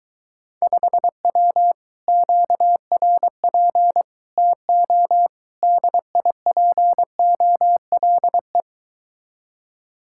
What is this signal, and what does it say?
Morse code, heard as a keyed tone, about 22 wpm, 700 Hz. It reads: 5W QRP TO DIPOLE